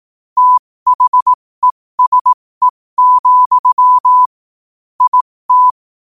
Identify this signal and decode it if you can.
Morse code, heard as a keyed tone, about 18 wpm, spaced as for 15 wpm, 1000 Hz. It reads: THESE, IT
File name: qotdfiles / 15wpm